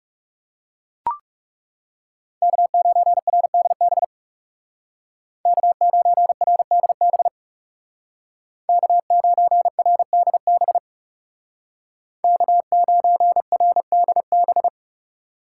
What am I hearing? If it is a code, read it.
Morse code, heard as a keyed tone, about 37 wpm, 700 Hz. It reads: K9RDB K9RDB K9RDB K9RDB
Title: Speed Racing - USA Call Signs 30wpm